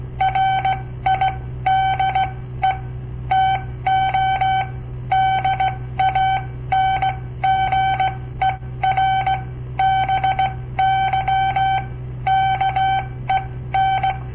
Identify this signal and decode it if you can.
Morse code, heard as a keyed tone, about 17 wpm, spaced as for 14 wpm, 770 Hz. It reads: RIDE TO DANGER BY KEN
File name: papertapes / local / Code Practice-01